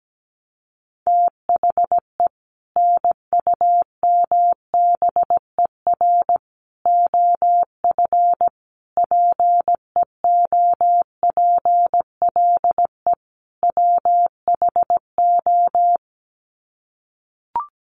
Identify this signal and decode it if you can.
Morse code, heard as a keyed tone, about 17 wpm, 700 Hz. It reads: THE NUMBER OF PEOPLE WHO
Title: Sets of 5 Words Encore 17wpm